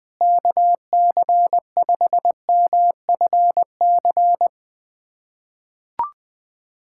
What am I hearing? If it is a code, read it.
Morse code, heard as a keyed tone, about 21 wpm, 700 Hz. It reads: KC5MFC
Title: USA Callsign DE Callsign - 6 Characters - 20wpm